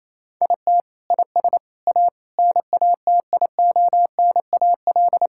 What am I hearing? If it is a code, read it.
Morse code, heard as a keyed tone, about 28 wpm, 700 Hz. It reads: IT IS A NATIONAL
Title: Sentences from Top 1000 Words - Review Part 5 - 28wpm